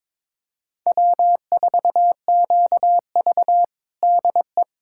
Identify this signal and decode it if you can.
Morse code, heard as a keyed tone, about 22 wpm, 700 Hz. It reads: W4QV DE
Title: USA Callsign DE Callsign - 4 Characters - 22wpm